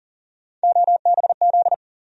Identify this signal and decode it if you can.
Morse code, heard as a keyed tone, about 40 wpm, 700 Hz. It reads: OBZ
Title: USA Call Sign Postfixes - 3 Letters 40wpm